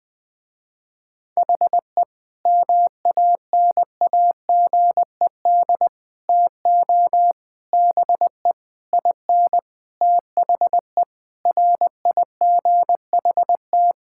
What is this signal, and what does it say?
Morse code, heard as a keyed tone, about 20 wpm, 700 Hz. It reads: HE MANAGED TO BE IN THE RIGHT
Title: Sentences with 9-Word Phrases - Spoken Beginning - 20wpm